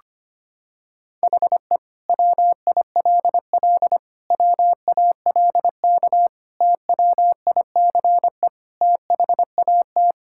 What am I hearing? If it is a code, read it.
Morse code, heard as a keyed tone, about 25 wpm, 700 Hz. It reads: HE WILL WALK TWICE THAT
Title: Sentences from Top 1600 Words 25wpm